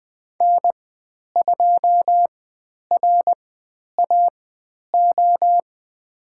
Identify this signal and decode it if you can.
Morse code, heard as a keyed tone, about 20 wpm, spaced as for 10 wpm, 700 Hz. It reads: N2RAO
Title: 095 Lesson 16g - Letter c - Call Signs 20wpm